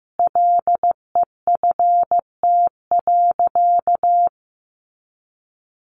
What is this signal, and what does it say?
Morse code, heard as a keyed tone, about 15 wpm, 700 Hz. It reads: LEFT.
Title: Sentences from Top 1200 Words 15wpm